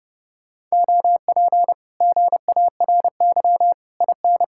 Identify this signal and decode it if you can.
Morse code, heard as a keyed tone, about 33 wpm, 700 Hz. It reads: OP GARY IN